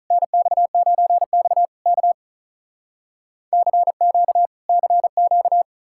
Morse code, heard as a keyed tone, about 46 wpm, 700 Hz. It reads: NX9X K CQ CQ